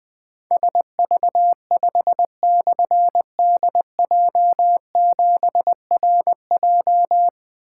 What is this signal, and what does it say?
Morse code, heard as a keyed tone, about 20 wpm, 700 Hz. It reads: SV5/DJ7RJ
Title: Difficult Call Signs 20wpm